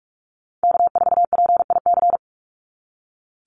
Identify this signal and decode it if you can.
Morse code, heard as a keyed tone, about 46 wpm, 690 Hz. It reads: K4PIC
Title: NA SOTA Chaser Callsigns 45wpm